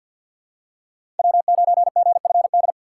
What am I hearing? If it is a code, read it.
Morse code, heard as a keyed tone, about 52 wpm, 700 Hz. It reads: W9GFD
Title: USA Call Signs - 5 Characters - 2x Character Spacing - 50wpm